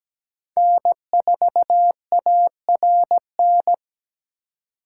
Morse code, heard as a keyed tone, about 20 wpm, 700 Hz. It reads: N4ARN
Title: Speed Racing - USA 1x3 Call Signs 17wpm